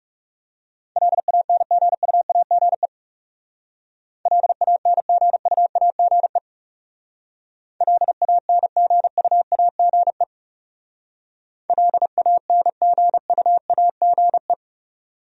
Morse code, heard as a keyed tone, about 36 wpm, 700 Hz. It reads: LANGUAGE LANGUAGE LANGUAGE LANGUAGE
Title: Speed-Racing Top 500 Words 30wpm